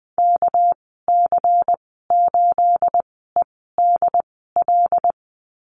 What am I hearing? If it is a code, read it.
Morse code, heard as a keyed tone, about 20 wpm, spaced as for 14 wpm, 700 Hz. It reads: KC8EDL
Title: 238 Lesson 38g - Letter q - Call Signs 20wpm